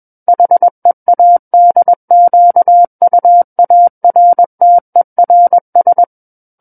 Morse code, heard as a keyed tone, about 21 wpm, 700 Hz. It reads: HEADQUARTERS
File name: Book One / 12-Chapter 12 - 21 WPM